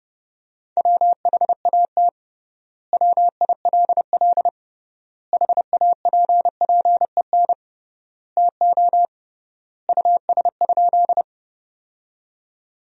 Morse code, heard as a keyed tone, about 28 wpm, 700 Hz. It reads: WHAT WILL HAPPEN TO US?
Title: 3x Word-Spacing - Sentences from Top 400 Words - 30wpm